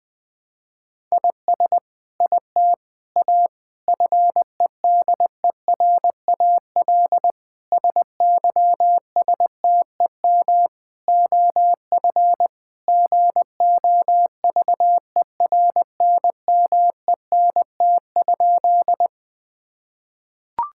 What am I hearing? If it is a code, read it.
Morse code, heard as a keyed tone, about 20 wpm, 700 Hz. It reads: IS IT A FEDERAL SYSTEM OF GOVERNMENT?
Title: Sentences from Top 1000 Words - Review Part 5 - 20wpm